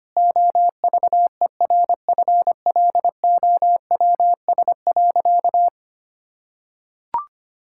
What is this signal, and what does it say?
Morse code, heard as a keyed tone, about 25 wpm, 700 Hz. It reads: OVERFLOWS.
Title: Pangram Challenge v2 - 25wpm